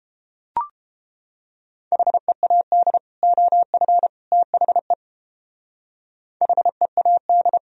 Morse code, heard as a keyed tone, about 31 wpm, 700 Hz. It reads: HEAD OF THE HEAD